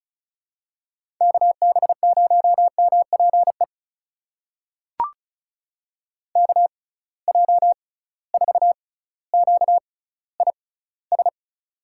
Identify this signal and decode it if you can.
Morse code, heard as a keyed tone, about 36 wpm, spaced as for 18 wpm, 700 Hz. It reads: KB0MPE KJ4QIS
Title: USA Call Signs - 6 Characters - 6x Character Spacing - 35wpm